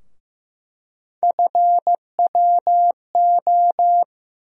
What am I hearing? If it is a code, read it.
Morse code, heard as a keyed tone, about 15 wpm, 700 Hz. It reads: FWO
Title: USA Call Sign Postfixes - 3 Letters 15wpm